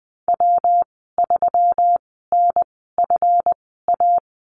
Morse code, heard as a keyed tone, about 20 wpm, spaced as for 14 wpm, 700 Hz. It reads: W3NFA
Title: 147 Lesson 24g - Letter y - Call Signs 20wpm